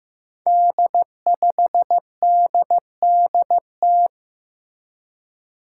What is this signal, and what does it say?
Morse code, heard as a keyed tone, about 15 wpm, 700 Hz. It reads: D5DDT
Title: Five Challenging Letter-Number Combinations 15wpm